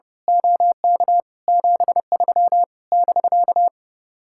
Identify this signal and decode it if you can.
Morse code, heard as a keyed tone, about 30 wpm, 700 Hz. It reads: OK 73 <BK>
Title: Speed-Racing Common QSO Phrases Encore - 30wpm